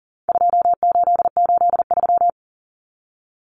Morse code, heard as a keyed tone, about 40 wpm, 700 Hz. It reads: 2883
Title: Common Cut Numbers - 4-Digit 40wpm